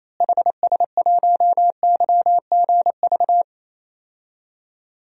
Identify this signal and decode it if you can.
Morse code, heard as a keyed tone, about 28 wpm, 700 Hz. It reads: HS1YGV
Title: DXCC-100 Call Signs - 6 Characters - Repeated 2x - 28wpm